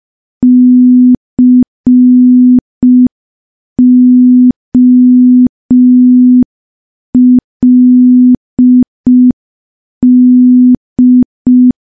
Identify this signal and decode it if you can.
Morse code, heard as a keyed tone, about 5 wpm, 250 Hz. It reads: COLD
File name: morse / morse-training-05WPM